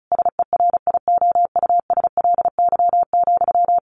Morse code, heard as a keyed tone, about 34 wpm, 700 Hz. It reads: SERIOUSLY,
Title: Sentences from Top 1200 Words 35wpm